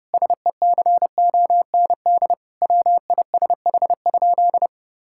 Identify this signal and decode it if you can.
Morse code, heard as a keyed tone, about 32 wpm, 700 Hz. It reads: SECOND WISH?
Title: Sentences from Top 1600 Words 30wpm